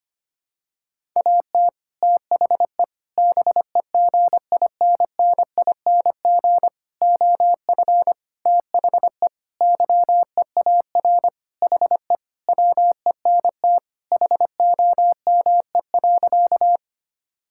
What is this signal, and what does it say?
Morse code, heard as a keyed tone, about 25 wpm, 700 Hz. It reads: AT THE BEGINNING OF THE YEAR HE WENT HOME.
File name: practice / 25wpm / Sentences with 4-Word Phrases - 25wpm